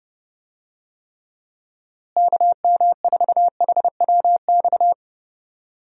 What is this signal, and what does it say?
Morse code, heard as a keyed tone, about 32 wpm, 700 Hz. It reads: KM4HWX
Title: USA Call Signs - 6 Characters - Repeated 2x - 30wpm